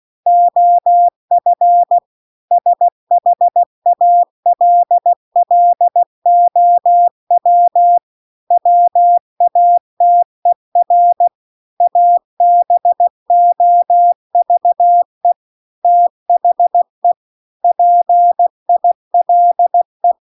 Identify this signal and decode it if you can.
Morse code, heard as a keyed tone, about 16 wpm, 700 Hz. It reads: OF SHALLOW WATER ABOVE THE PILE